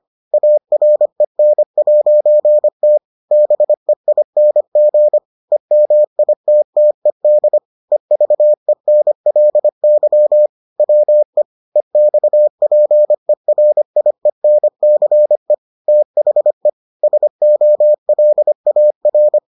Morse code, heard as a keyed tone, about 25 wpm, 600 Hz. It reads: AREN'T BEING EMITTED EVENLY WE EXPERIENCE THE SOLAR